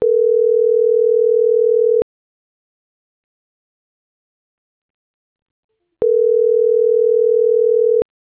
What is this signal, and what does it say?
Ringback tone (North America), 440 + 480 Hz.